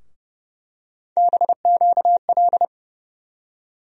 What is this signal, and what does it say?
Morse code, heard as a keyed tone, about 30 wpm, 700 Hz. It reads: BQL
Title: USA Call Sign Postfixes - 3 Letters 30wpm